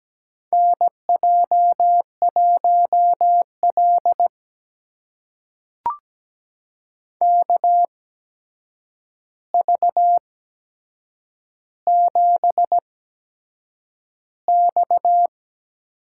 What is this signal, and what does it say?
Morse code, heard as a keyed tone, about 17 wpm, spaced as for 7 wpm, 700 Hz. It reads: NJ1L KV7X
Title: USA Call Signs - 4 Characters - 8x Character Spacing - 17wpm